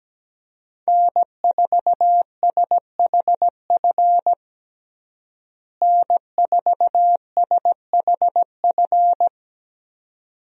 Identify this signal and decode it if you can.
Morse code, heard as a keyed tone, about 17 wpm, 700 Hz. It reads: N4SHF N4SHF
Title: USA Call Signs - 5 Characters - Repeated 2x - 17wpm